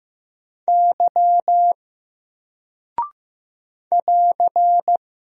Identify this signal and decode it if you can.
Morse code, heard as a keyed tone, about 15 wpm, 700 Hz. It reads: Y <AR>
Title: Single Character (Letter:Number:Punctuation:Prosign) 15wpm